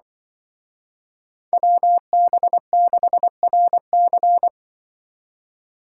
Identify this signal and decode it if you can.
Morse code, heard as a keyed tone, about 24 wpm, 700 Hz. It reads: WB6RC
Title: Speed Racing - USA Call Signs 20wpm